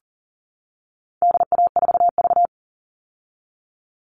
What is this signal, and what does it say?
Morse code, heard as a keyed tone, about 40 wpm, 700 Hz. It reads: DA4V
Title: All Cut Numbers - 4-Digit 40wpm